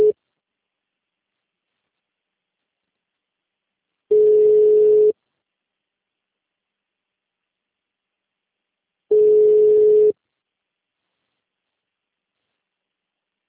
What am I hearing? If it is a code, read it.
Ringback tone (Europe, 425 Hz), with speech.